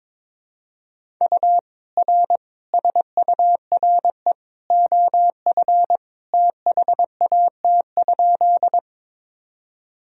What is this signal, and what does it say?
Morse code, heard as a keyed tone, about 22 wpm, 700 Hz. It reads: U R SURE OF THAT?
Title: Sentences from Top 300 Words and Abbreviations 22wpm